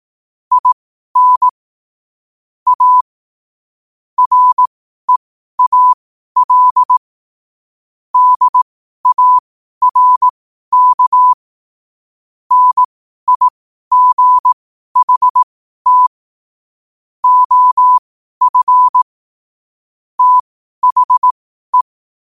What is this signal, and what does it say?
Morse code, heard as a keyed tone, about 18 wpm, spaced as for 12 wpm, 1000 Hz. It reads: IN A REAL DARK NIGHT OF THE
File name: qotdfiles / 12wpm